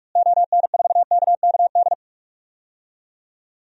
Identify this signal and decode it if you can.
Morse code, heard as a keyed tone, about 44 wpm, 700 Hz. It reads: ON4KKD
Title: DXCC-100 Call Signs - 6 Characters - Repeated 2x - 45wpm